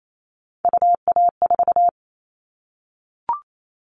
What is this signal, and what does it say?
Morse code, heard as a keyed tone, about 28 wpm, 700 Hz. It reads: UA4
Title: Callsign Prefixes for Top 100 DXCC - 28wpm